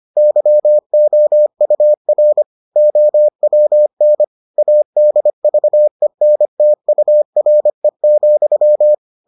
Morse code, heard as a keyed tone, about 25 wpm, 600 Hz. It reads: YOUR OWN ADVENTURE,